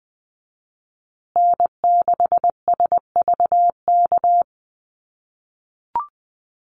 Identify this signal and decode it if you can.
Morse code, heard as a keyed tone, about 20 wpm, 700 Hz. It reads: N6SVK
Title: USA Call Signs Repeated 2x 20wpm